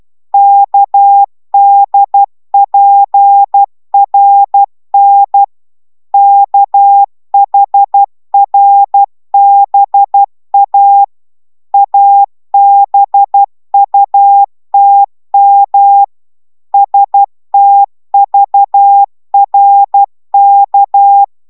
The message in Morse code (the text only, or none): KDPRN KHRBA ABUTM STVRK